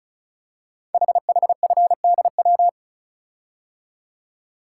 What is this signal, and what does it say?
Morse code, heard as a keyed tone, about 36 wpm, 700 Hz. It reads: HHFDW
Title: Code Groups - 5 Letters - 35wpm